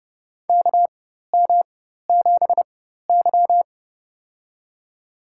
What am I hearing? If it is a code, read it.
Morse code, heard as a keyed tone, about 30 wpm, spaced as for 14 wpm, 700 Hz. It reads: KM7Y
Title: USA Call Signs - 4 Characters - 4x Character Spacing - 30wpm